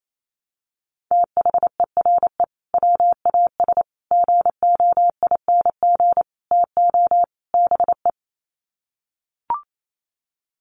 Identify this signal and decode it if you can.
Morse code, heard as a keyed tone, about 28 wpm, 700 Hz. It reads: THERE WAS GOING TO BE
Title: Sets of 5 Words Encore2 28wpm